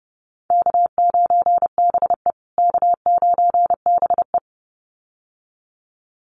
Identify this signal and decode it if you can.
Morse code, heard as a keyed tone, about 30 wpm, 700 Hz. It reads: K9BE K9BE
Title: USA Call Signs Repeated 2x 30wpm